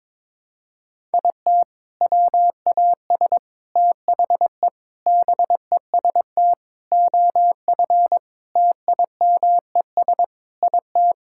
Morse code, heard as a keyed tone, about 22 wpm, 700 Hz. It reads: IT WAS THE BEST OF TIMES IT
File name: practice / 22wpm / Sentences with 8-Word Phrases - 22wpm